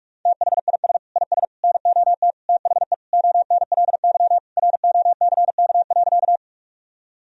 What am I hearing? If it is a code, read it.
Morse code, heard as a keyed tone, about 46 wpm, 700 Hz. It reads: THIS IS NOT THE ONLY ROCK.
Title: Sentences from Top 1000 Words - Review Part 2 - 45wpm